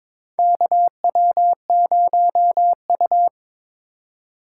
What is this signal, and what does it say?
Morse code, heard as a keyed tone, about 22 wpm, 700 Hz. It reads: KW0U
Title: USA Call Signs - 4 Characters - 4x Character Spacing - 22wpm